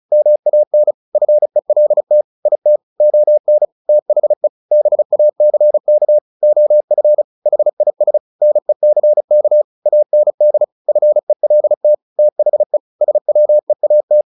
Morse code, heard as a keyed tone, about 34 wpm, 600 Hz. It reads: MAN FELT IT ON THE BACK OF HIS NECK AND FELT THE SWEAT
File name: the old man and the sea / the old man and the sea 1 35wpm